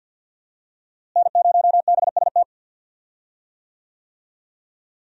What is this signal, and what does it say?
Morse code, heard as a keyed tone, about 51 wpm, 700 Hz. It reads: N0BST
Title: USA Call Signs - 5 Characters - Repeated 2x - 50wpm